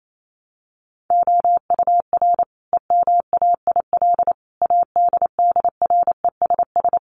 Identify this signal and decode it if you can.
Morse code, heard as a keyed tone, about 28 wpm, 700 Hz. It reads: OUR EMAIL ADDRESS
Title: Sets of 5 Words Encore2 28wpm